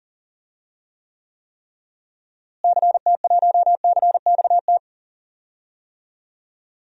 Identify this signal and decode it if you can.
Morse code, heard as a keyed tone, about 37 wpm, 700 Hz. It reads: CT1CXT
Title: DXCC-100 Call Signs - 6 Characters - Repeated 3x - 40wpm